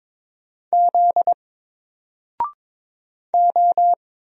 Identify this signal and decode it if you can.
Morse code, heard as a keyed tone, about 22 wpm, 700 Hz. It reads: Z O
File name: Single-Letter-Number-22wpm